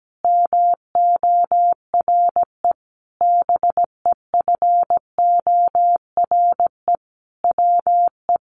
Morse code, heard as a keyed tone, about 17 wpm, 700 Hz. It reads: MORE BEFORE WE
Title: Sentences from Top 1500 Words 17wpm